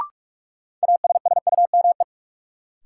Morse code, heard as a keyed tone, about 46 wpm, 700 Hz. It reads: ASSUME
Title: Sentences from Top 1500 Words - New Word Drill Alt - 45wpm